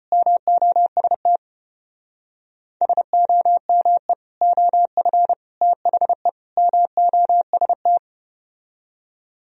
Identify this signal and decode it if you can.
Morse code, heard as a keyed tone, about 33 wpm, 700 Hz. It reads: MOST SOME OF THE MOST